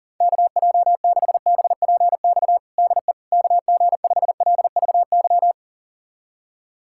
Morse code, heard as a keyed tone, about 40 wpm, 700 Hz. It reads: KJ6BPX DE KG5LVY
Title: USA Callsign DE Callsign - 6 Characters - 40wpm